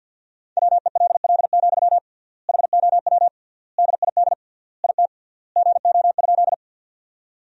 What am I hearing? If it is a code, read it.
Morse code, heard as a keyed tone, about 50 wpm, 700 Hz. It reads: WELL, HOW DID IT GO?
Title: 3x Word-Spacing - Sentences from Top 200 Words - 50wpm